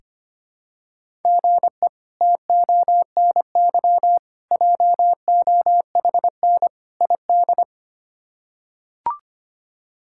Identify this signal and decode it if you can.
Morse code, heard as a keyed tone, about 25 wpm, 700 Hz. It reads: GE TONY JOHN ID
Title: K1USN SST QSO Practice - 25wpm